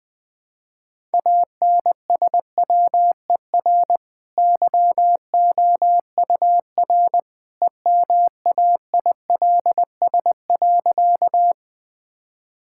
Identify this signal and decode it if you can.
Morse code, heard as a keyed tone, about 20 wpm, 700 Hz. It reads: ANSWER YOUR EMAILS.
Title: Easy Sentences with Hints - 20wpm